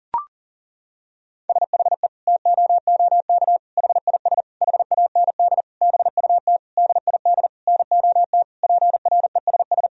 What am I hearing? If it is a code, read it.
Morse code, heard as a keyed tone, about 37 wpm, 700 Hz. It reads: SHE TOOK HIS HAND BUT DID NOT PRESS
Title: Sentences from Top 1000 Words - Review Part 1 - 40wpm